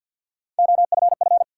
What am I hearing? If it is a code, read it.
Morse code, heard as a keyed tone, about 50 wpm, 700 Hz. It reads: OFF
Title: Sentences from Top 200 Words - New Word Drill - 50wpm